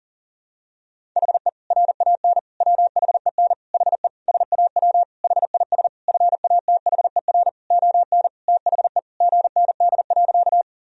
Morse code, heard as a keyed tone, about 40 wpm, 700 Hz. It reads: HE RAN WHEN HE SAW HIS FATHER ON THE GND.
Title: Sentences from Top 500 Words and Abbreviations 40wpm